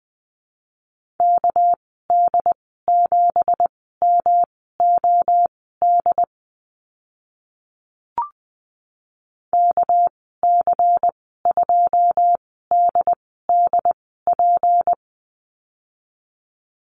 Morse code, heard as a keyed tone, about 20 wpm, spaced as for 14 wpm, 700 Hz. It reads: KD7MOD KC2DDP
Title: 219 Lesson 35g - Number 0 - Call Signs 20wpm